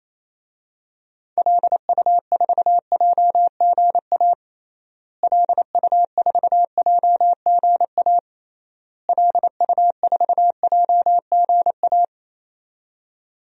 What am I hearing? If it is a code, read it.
Morse code, heard as a keyed tone, about 28 wpm, 700 Hz. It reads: LU4JGA LU4JGA LU4JGA
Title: DXCC-100 Call Signs - 6 Characters - Repeated 3x - 28wpm